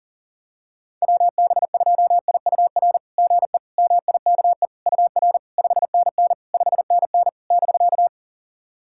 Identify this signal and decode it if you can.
Morse code, heard as a keyed tone, about 40 wpm, 700 Hz. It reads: WB2IUR GE MIKE UR 5NN 5NN <BK>
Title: SOTA QSO Practice 40wpm